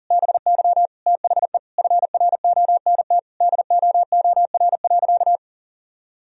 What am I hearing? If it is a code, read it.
Morse code, heard as a keyed tone, about 40 wpm, 700 Hz. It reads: BY THE FRONT DOOR.
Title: Sentences from Top 1000 Words - Review Part 2 - 40wpm